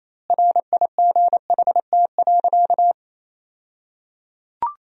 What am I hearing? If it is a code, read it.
Morse code, heard as a keyed tone, about 28 wpm, 700 Hz. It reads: RIGHT.
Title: 2x Word-Spacing - Sentences from Top 300 Words - 28wpm